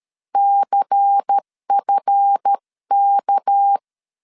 Morse code, heard as a keyed tone, about 13 wpm, 800 Hz. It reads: CFK
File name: LEZIONI / 7 GF9Z / 7 GF 12wpm 2122